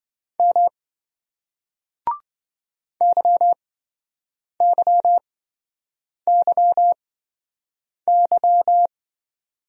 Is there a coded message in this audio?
Yes: Morse code from 0.4 to 8.9 s, heard as a keyed tone, about 27 wpm, spaced as for 7 wpm, 700 Hz. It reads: M YYYY